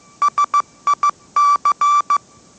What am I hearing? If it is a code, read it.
Morse code, heard as a keyed tone, about 15 wpm, 1160 Hz. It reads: SIC
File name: mrx / mrx065 16wpm